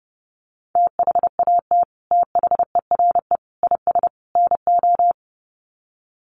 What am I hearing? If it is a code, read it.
Morse code, heard as a keyed tone, about 30 wpm, 700 Hz. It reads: THAT THERE IS NO
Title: Sets of 4 Words 30wpm